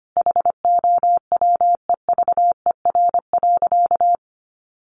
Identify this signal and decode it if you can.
Morse code, heard as a keyed tone, about 25 wpm, 700 Hz. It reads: HOWEVER.
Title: Sentences from Top 1100 Words 25wpm